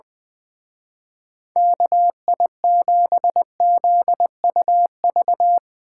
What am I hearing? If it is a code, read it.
Morse code, heard as a keyed tone, about 20 wpm, 700 Hz. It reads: KI7ZUV
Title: Speed Racing - USA 2x3 Call Signs 20wpm